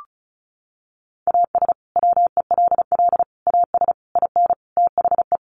Morse code, heard as a keyed tone, about 34 wpm, 700 Hz. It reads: AS WELL AS IN THE
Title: Sets of 5 Words Encore 35wpm